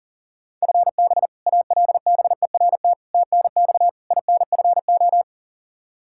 Morse code, heard as a keyed tone, about 40 wpm, 700 Hz. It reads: FB ALBERT TNX INFO
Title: Practice QSO - Exchange 3 - More Info - 40wpm